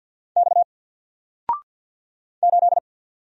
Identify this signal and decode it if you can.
Morse code, heard as a keyed tone, about 50 wpm, 700 Hz. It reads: X 8
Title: Single Letter-Number 50wpm